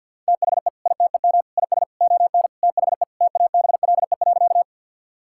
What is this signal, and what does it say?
Morse code, heard as a keyed tone, about 50 wpm, 700 Hz. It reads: THE ITEM IS ON THE TABLE.
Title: Sentences from Top 1000 Words - Review Part 6 - 50wpm